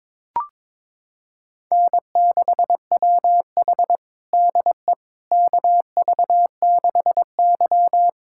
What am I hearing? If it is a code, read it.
Morse code, heard as a keyed tone, about 22 wpm, 700 Hz. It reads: N6WH DE KV6Y
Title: USA Callsign DE Callsign - 4 Characters - 22wpm